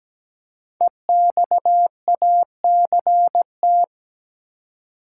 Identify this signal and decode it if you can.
Morse code, heard as a keyed tone, about 17 wpm, 700 Hz. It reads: EXACT
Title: Five Letter Words - 17wpm